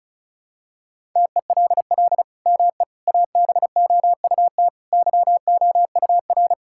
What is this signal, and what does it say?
Morse code, heard as a keyed tone, about 36 wpm, 700 Hz. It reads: TELL ME ABOUT YOUR
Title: Sentences from Top 1600 Words 35wpm